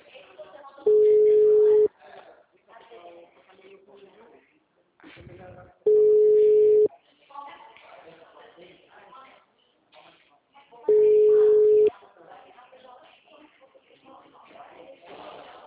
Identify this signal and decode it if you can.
Ringback tone (Europe, 425 Hz), with speech.